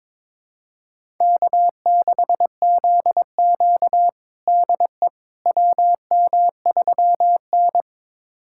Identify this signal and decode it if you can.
Morse code, heard as a keyed tone, about 22 wpm, 700 Hz. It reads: K6ZQ DE WM3N
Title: USA Callsign DE Callsign - 4 Characters - 22wpm